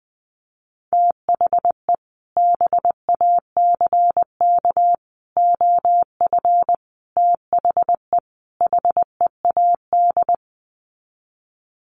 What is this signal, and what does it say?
Morse code, heard as a keyed tone, about 20 wpm, 700 Hz. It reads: THE BACK OF THE HEAD
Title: Sets of 5 Words Encore2 20wpm